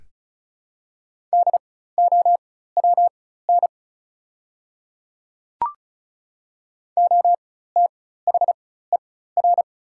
Morse code, heard as a keyed tone, about 34 wpm, spaced as for 16 wpm, 700 Hz. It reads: DOWN OTHER
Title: 4x Character Spacing - Top 200 Words - 35wpm